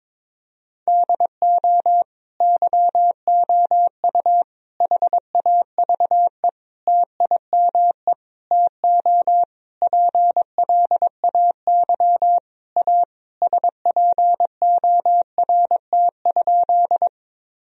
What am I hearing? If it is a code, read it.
Morse code, heard as a keyed tone, about 22 wpm, 700 Hz. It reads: DO YOU HAVE TIME TO PLAY A SPORT?
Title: Sentences from Top 1000 Words - Review Part 4 - 22wpm